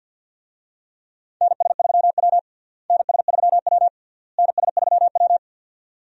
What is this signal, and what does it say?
Morse code, heard as a keyed tone, about 52 wpm, 700 Hz. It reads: NS3W NS3W NS3W
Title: USA Call Signs - 4 Characters - Repeated 3x - 50wpm